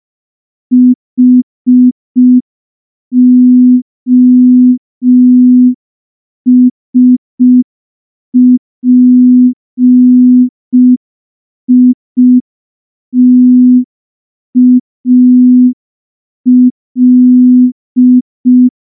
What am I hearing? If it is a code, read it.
Morse code, heard as a keyed tone, about 5 wpm, 250 Hz. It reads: HOSPITAL
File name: morse / morse-training-05WPM